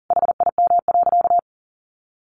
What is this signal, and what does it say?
Morse code, heard as a keyed tone, about 40 wpm, 700 Hz. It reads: HIM.
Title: Sentences from Top 400 Words 40wpm